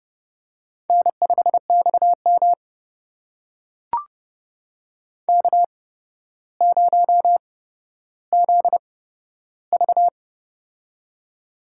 Morse code, heard as a keyed tone, about 30 wpm, spaced as for 13 wpm, 700 Hz. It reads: N5XM K0ZV
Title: USA Call Signs - 4 Characters - 8x Character Spacing - 30wpm